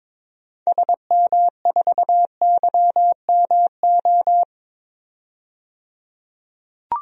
Morse code, heard as a keyed tone, about 22 wpm, 700 Hz. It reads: SM4YMO
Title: DXCC-100 Call Signs - 6 Characters - 22wpm